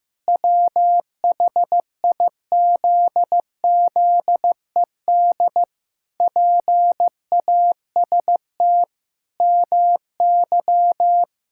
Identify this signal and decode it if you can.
Morse code, heard as a keyed tone, about 15 wpm, 700 Hz. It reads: WHIZZED PAST MY
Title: Pangram Challenge v2 - 15wpm